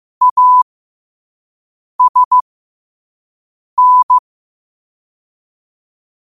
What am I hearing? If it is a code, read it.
Morse code, heard as a keyed tone, about 15 wpm, spaced as for 5 wpm, 1000 Hz. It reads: ASN